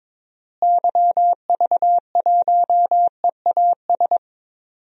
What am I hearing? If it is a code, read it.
Morse code, heard as a keyed tone, about 22 wpm, 700 Hz. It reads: YV1EAS
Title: DXCC-100 Call Signs - 6 Characters - Repeated 2x - 22wpm